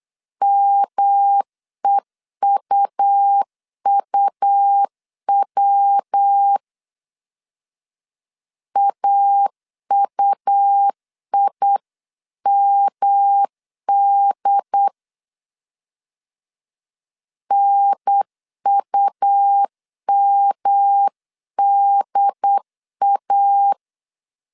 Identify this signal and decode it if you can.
Morse code, heard as a keyed tone, about 8 wpm, 800 Hz. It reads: MEUUW AUIMD NUMDA